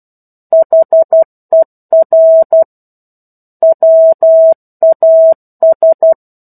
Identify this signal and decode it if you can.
Morse code, heard as a keyed tone, about 12 wpm, 650 Hz. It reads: HER WAS
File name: 01-Chapter-01 12 WPM